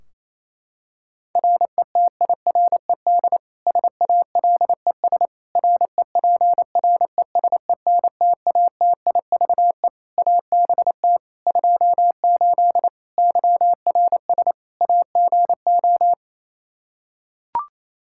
Morse code, heard as a keyed tone, about 28 wpm, 700 Hz. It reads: RETIRED SALES REPRESENTATIVE ABT 28 YRS AGO
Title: Practice QSO - Exchange 3 - More Info - 28wpm